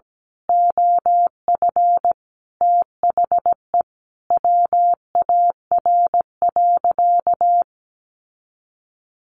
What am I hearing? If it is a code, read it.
Morse code, heard as a keyed tone, about 17 wpm, 700 Hz. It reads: OF THE WAR.
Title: Sentences from Top 1400 Words 17wpm